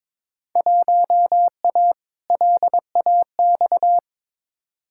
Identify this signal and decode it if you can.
Morse code, heard as a keyed tone, about 23 wpm, 700 Hz. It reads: 1A LAX
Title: Field Day Exchanges - 22wpm